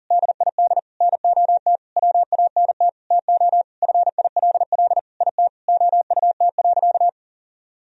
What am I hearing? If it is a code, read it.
Morse code, heard as a keyed tone, about 40 wpm, 700 Hz. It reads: DID NOT WANT TO FILL IT OUT.
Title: Sentences from Top 1000 Words - Review Part 3 - 40wpm